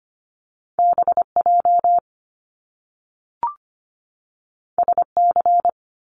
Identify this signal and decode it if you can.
Morse code, heard as a keyed tone, about 25 wpm, 700 Hz. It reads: BJ SC
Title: Top Two Letter Combinations in Words 25wpm